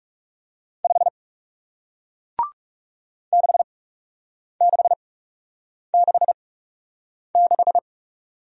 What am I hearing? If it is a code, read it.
Morse code, heard as a keyed tone, about 38 wpm, spaced as for 8 wpm, 700 Hz. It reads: 5 6666